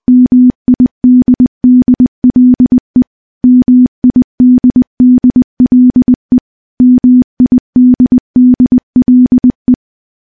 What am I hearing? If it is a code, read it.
Morse code, heard as a keyed tone, about 20 wpm, 250 Hz. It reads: MIDDLE MIDDLE MIDDLE
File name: morse-training-20WPM